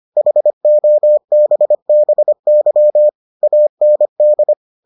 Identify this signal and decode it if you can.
Morse code, heard as a keyed tone, about 25 wpm, 600 Hz. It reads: HOBBY AND